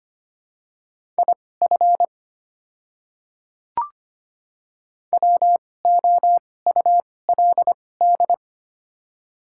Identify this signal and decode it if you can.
Morse code, heard as a keyed tone, about 25 wpm, spaced as for 18 wpm, 700 Hz. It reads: IF WOULD
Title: 2x Character Spacing - Top 200 Words - 25wpm